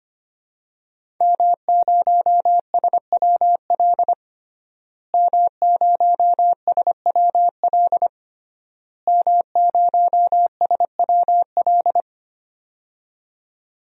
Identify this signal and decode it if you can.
Morse code, heard as a keyed tone, about 25 wpm, 700 Hz. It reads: M0SWL M0SWL M0SWL
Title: DXCC-100 Call Signs - 5 Characters - Repeated 3x - 25wpm